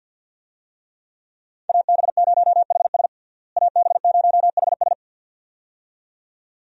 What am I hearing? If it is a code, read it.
Morse code, heard as a keyed tone, about 52 wpm, 700 Hz. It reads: AB0HS AB0HS
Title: USA Call Signs - 5 Characters - Repeated 2x - 50wpm